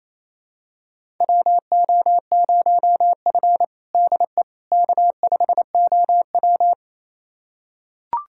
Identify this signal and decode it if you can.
Morse code, heard as a keyed tone, about 28 wpm, 700 Hz. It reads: WO0F DE K5OW
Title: USA Callsign DE Callsign - 4 Characters - 28wpm